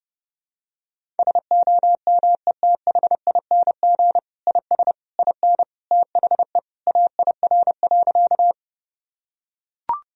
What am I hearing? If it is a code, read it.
Morse code, heard as a keyed tone, about 32 wpm, 700 Hz. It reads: SOMETHING IS IN THE AIR.
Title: Sentences from Top 1000 Words - Review Part 1 - 30wpm